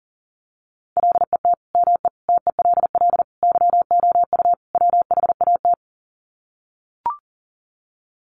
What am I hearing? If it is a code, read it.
Morse code, heard as a keyed tone, about 40 wpm, 700 Hz. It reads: LET ME TELL YOU WHAT
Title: Sets of 5 Words Encore 40wpm